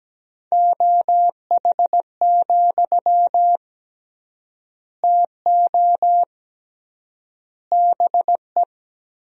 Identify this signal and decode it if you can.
Morse code, heard as a keyed tone, about 17 wpm, 700 Hz. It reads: OH, TO BE
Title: 3x Word-Spacing - Sentences from Top 500 Words - 17wpm